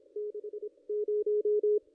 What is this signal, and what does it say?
Morse code, heard as a keyed tone, about 25 wpm, spaced as for 21 wpm, 440 Hz, with radio noise. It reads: 60
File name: contests / ss-cw-k5tr-op-n5rz -1700